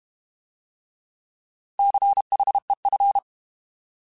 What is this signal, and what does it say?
Morse code, heard as a keyed tone, about 32 wpm, 800 Hz. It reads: CHEF
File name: Alphabetical / Alphabetical mp3s / Cs @30